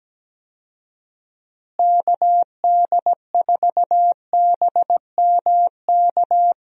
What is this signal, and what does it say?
Morse code, heard as a keyed tone, about 17 wpm, 700 Hz. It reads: KD4BMK
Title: USA Call Signs - 6 Characters - Repeated 3x - 17wpm